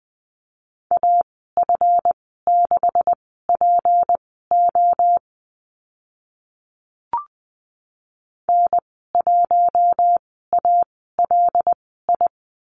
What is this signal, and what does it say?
Morse code, heard as a keyed tone, about 20 wpm, spaced as for 14 wpm, 700 Hz. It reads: AF6PO N1ALI
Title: 186 Lesson 30g - Letter b - Call Signs 20wpm